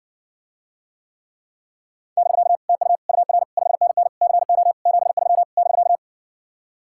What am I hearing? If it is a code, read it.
Morse code, heard as a keyed tone, about 60 wpm, 700 Hz. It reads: <BK> TU UR 5NN CO 73 <BK>